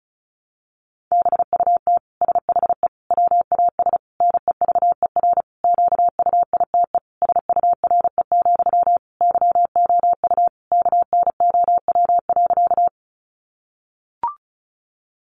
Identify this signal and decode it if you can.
Morse code, heard as a keyed tone, about 34 wpm, 700 Hz. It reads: BUT SHE WAS NEVER QUITE SURE, YOU KNOW.
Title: Sentences from Top 700 Words 35wpm